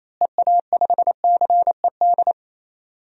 Morse code, heard as a keyed tone, about 28 wpm, 700 Hz. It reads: EA5CED
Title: DXCC-100 Call Signs - 6 Characters - Repeated 3x - 28wpm